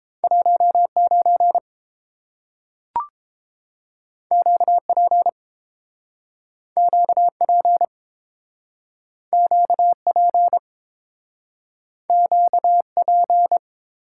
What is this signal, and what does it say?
Morse code, heard as a keyed tone, about 30 wpm, 700 Hz. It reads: 19 QP QP QP QP